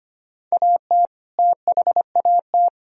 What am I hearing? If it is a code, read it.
Morse code, heard as a keyed tone, about 25 wpm, 700 Hz. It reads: AT THAT
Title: Sentences from Top 1000 Words - Review Part 1 - 25wpm